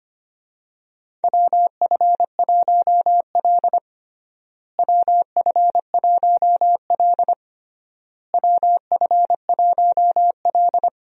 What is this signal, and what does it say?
Morse code, heard as a keyed tone, about 25 wpm, 700 Hz. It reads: WF1L WF1L WF1L
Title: USA Call Signs - 4 Characters - Repeated 3x - 25wpm